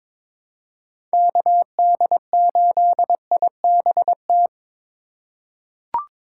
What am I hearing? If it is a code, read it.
Morse code, heard as a keyed tone, about 21 wpm, 700 Hz. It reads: KD8IBT
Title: USA Call Signs - 4x Character Spacing - 22wpm